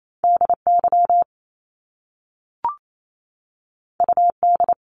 Morse code, heard as a keyed tone, about 28 wpm, 700 Hz. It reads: DY UD
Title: Top Two Letter Combinations in Words 28wpm